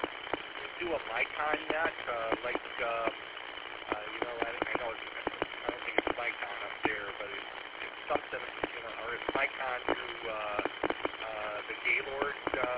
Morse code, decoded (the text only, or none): none